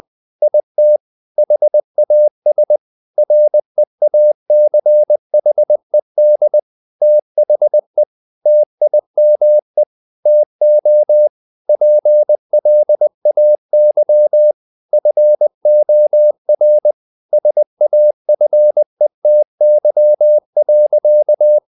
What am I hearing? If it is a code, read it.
Morse code, heard as a keyed tone, about 20 wpm, 600 Hz. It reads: IT HAS REACHED THE TIME TO PLAY FOR SAFETY.
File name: the old man and the sea part 2 20wpm